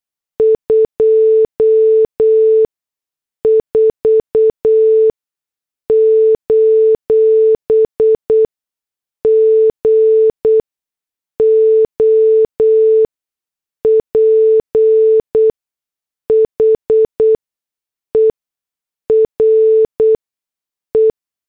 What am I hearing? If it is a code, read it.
Morse code, heard as a keyed tone, about 8 wpm, spaced as for 6 wpm, 440 Hz. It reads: 24: GOPHER E